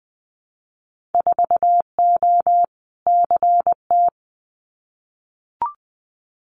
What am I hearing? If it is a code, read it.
Morse code, heard as a keyed tone, about 20 wpm, 700 Hz. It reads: 4O CT
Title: Winter Field Day Exchanges Encore 20wpm